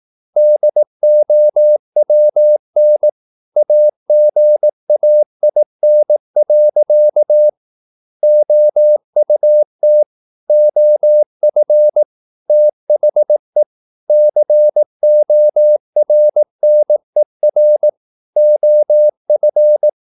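Morse code, heard as a keyed tone, about 18 wpm, 600 Hz. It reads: DOWN AGAIN. OUT OF THE CORNER OF
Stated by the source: Morse practice at 18 wpm, English text, from CW med Gnister.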